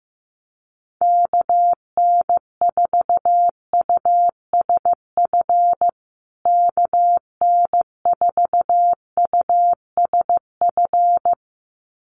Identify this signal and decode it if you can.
Morse code, heard as a keyed tone, about 15 wpm, 700 Hz. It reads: KN4USF KN4USF
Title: USA Call Signs Repeated 2x 15wpm